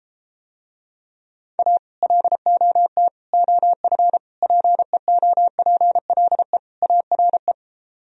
Morse code, heard as a keyed tone, about 33 wpm, 700 Hz. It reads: A LOT OF PEOPLE ARE